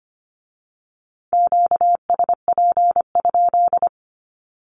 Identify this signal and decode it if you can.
Morse code, heard as a keyed tone, about 25 wpm, 700 Hz. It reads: QSP?
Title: Q-Codes with Spoken Meaning - 25wpm